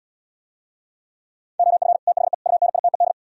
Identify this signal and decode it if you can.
Morse code, heard as a keyed tone, about 77 wpm, 700 Hz. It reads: OF THE UNITED